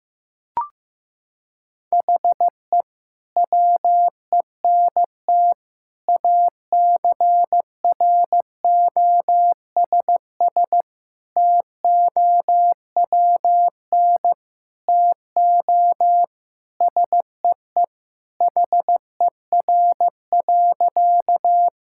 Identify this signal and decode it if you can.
Morse code, heard as a keyed tone, about 16 wpm, 700 Hz. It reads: HE WENT ACROSS TOWN TO SEE HER.
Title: Sentences from Top 1000 Words - Review Part 5 - 15wpm